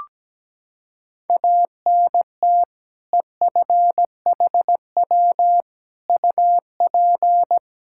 Morse code, heard as a keyed tone, about 17 wpm, 700 Hz. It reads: ANT EFHW UP
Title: Practice QSO - Exchange 2 - Info - 17wpm